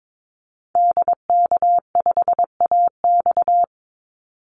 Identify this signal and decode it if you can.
Morse code, heard as a keyed tone, about 22 wpm, 700 Hz. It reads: DK5AX
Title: Call Signs Repeated 3x - 22wpm